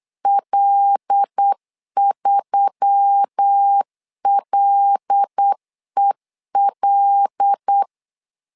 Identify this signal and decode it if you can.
Morse code, heard as a keyed tone, about 8 wpm, 800 Hz. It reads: L3LEL
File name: LEZIONI / 5 LBVC / 5 L 8wpm 2831